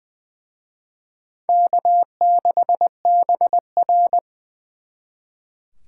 Morse code, heard as a keyed tone, about 23 wpm, 690 Hz. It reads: K6BR
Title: Speed Racing - USA 1x2 Call Signs 20wpm